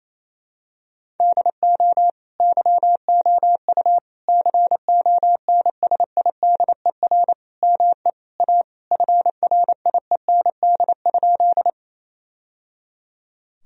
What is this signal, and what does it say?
Morse code, heard as a keyed tone, about 28 wpm, 700 Hz. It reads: DO YOU CONSIDER ME A FRIEND?
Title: Sentences from Top 1000 Words - Review Part 4 - 28wpm